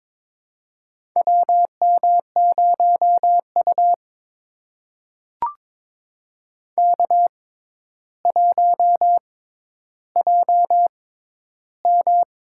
Morse code, heard as a keyed tone, about 23 wpm, spaced as for 11 wpm, 700 Hz. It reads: WM0U K1JM
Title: USA Call Signs - 4 Characters - 6x Character Spacing - 22wpm